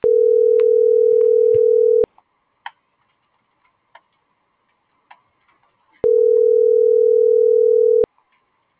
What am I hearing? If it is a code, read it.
Ringback tone (North America), 440 + 480 Hz.
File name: CENTRAL GENERAL / out karla.sky - .17463